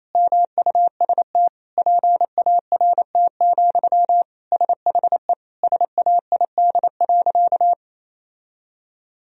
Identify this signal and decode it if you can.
Morse code, heard as a keyed tone, about 28 wpm, 700 Hz. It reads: MUST PART, SHE SAID.
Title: Sentences from Top 1000 Words - Review Part 1 - 28wpm